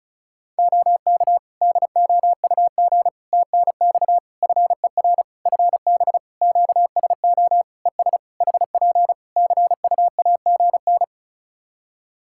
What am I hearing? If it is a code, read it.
Morse code, heard as a keyed tone, about 36 wpm, 700 Hz. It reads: OK DOUG TNX FER FB QSO ES HP CUAGN
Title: K1USN SST QSO Practice 35wpm